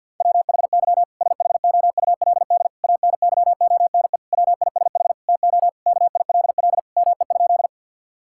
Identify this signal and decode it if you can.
Morse code, heard as a keyed tone, about 50 wpm, 700 Hz. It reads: WHY SHOULD ANYONE WISH TO KILL ME?
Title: Sentences from Top 1600 Words 50wpm